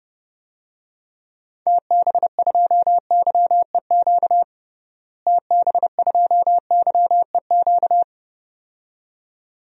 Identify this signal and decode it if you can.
Morse code, heard as a keyed tone, about 32 wpm, 700 Hz. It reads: TB2YEQ TB2YEQ
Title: DXCC-100 Call Signs - 6 Characters - Repeated 2x - 30wpm